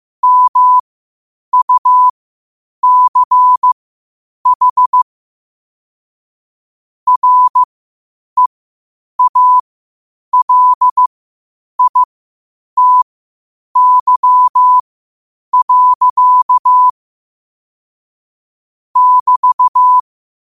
Morse code, heard as a keyed tone, about 15 wpm, spaced as for 8 wpm, 1000 Hz. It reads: MUCH REALITY. <BT>
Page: Quotes for Thu, 14 Aug 2025 in Morse Code at 8 words per minute.